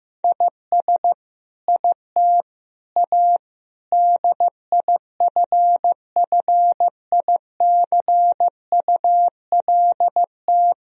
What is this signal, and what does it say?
Morse code, heard as a keyed tone, about 15 wpm, 700 Hz. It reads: IS IT A DIFFICULT
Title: Sentences from Top 1000 Words - Review Part 2 - 15wpm